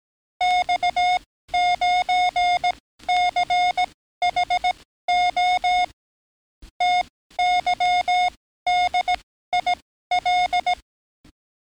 Morse code, heard as a keyed tone, about 18 wpm, spaced as for 13 wpm, 720 Hz. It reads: X9CHO TYDIL
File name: training / NAVPERS 11049to11060 / NAVPERS 11056 RA RB